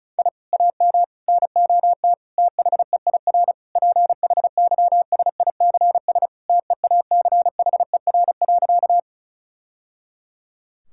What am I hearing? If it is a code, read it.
Morse code, heard as a keyed tone, about 36 wpm, 700 Hz. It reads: I AM NOT THEIR PHYSICS TEACHER.
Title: Sentences from Top 1000 Words - Review Part 6 - 35wpm